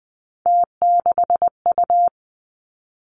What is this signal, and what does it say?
Morse code, heard as a keyed tone, about 20 wpm, 700 Hz. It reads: T6U
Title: Three Challenging Letter-Number Combinations 20wpm